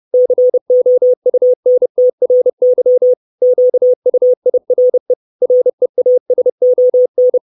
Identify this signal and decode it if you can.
Morse code, heard as a keyed tone, about 30 wpm, 500 Hz. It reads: COUNTRY QUIRE REASON
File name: 632 Words 30WPM